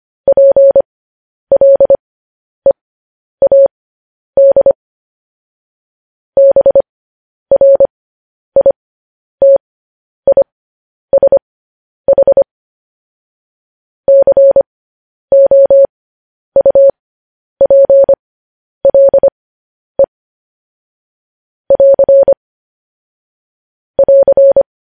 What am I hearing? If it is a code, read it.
Morse code, heard as a keyed tone, about 25 wpm, spaced as for 10 wpm, 550 Hz. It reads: PLEAD BRITISH COUPLE <AR> <AR>
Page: news headlines in morse code at 10 wpm